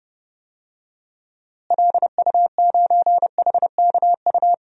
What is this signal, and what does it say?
Morse code, heard as a keyed tone, about 30 wpm, 700 Hz. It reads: LU9HKU
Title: DXCC-100 Call Signs - 6 Characters - Repeated 3x - 30wpm